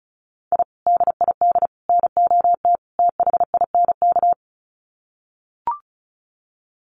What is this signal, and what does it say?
Morse code, heard as a keyed tone, about 34 wpm, 700 Hz. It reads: I DID NOT THINK
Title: Sets of 4 Words 35wpm